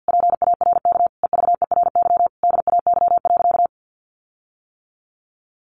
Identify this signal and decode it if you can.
Morse code, heard as a keyed tone, about 50 wpm, 700 Hz. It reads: PARK EVERY DAY.
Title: Sentences from Top 1300 Words 50wpm